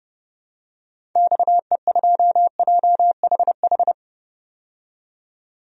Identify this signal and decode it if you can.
Morse code, heard as a keyed tone, about 30 wpm, 700 Hz. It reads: XE2JHH
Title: DXCC-100 Call Signs - 6 Characters - 30wpm